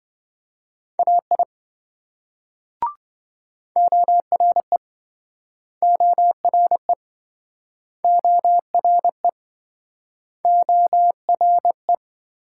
Morse code, heard as a keyed tone, about 26 wpm, 700 Hz. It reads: AI ORE ORE ORE ORE